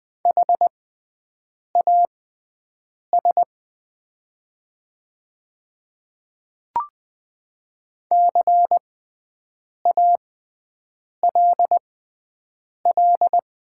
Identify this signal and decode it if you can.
Morse code, heard as a keyed tone, about 20 wpm, spaced as for 7 wpm, 700 Hz. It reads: HAS CALL
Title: 6x Character Spacing - Top 100 Words - 20wpm